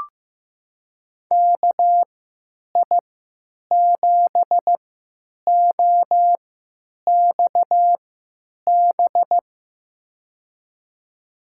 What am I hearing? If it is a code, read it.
Morse code, heard as a keyed tone, about 15 wpm, spaced as for 9 wpm, 700 Hz. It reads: KI7OXB
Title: USA Call Signs - 6 Characters - 3x Character Spacing - 15wpm